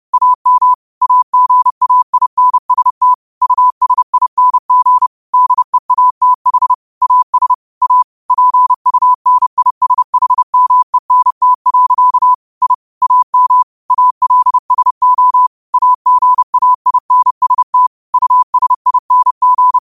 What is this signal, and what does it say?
Morse code, heard as a keyed tone, about 30 wpm, 1000 Hz. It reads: AM AGAINST USING DEATH AS A PUNISHMENT. I AM ALSO AGAINST USING